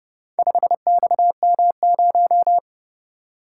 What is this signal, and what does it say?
Morse code, heard as a keyed tone, about 29 wpm, 700 Hz. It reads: 5XM0